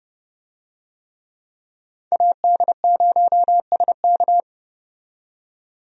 Morse code, heard as a keyed tone, about 31 wpm, 700 Hz. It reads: AD0SK
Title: USA Call Signs - 5 Characters - 30wpm